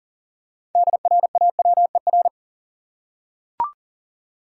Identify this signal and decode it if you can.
Morse code, heard as a keyed tone, about 40 wpm, 700 Hz. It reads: DRAWER
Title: Six Letter Words - 40wpm